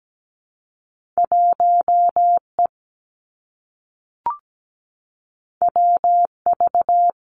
Morse code, heard as a keyed tone, about 17 wpm, 700 Hz. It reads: 1E WV
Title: Two Challenging Letter-Number Combinations 17wpm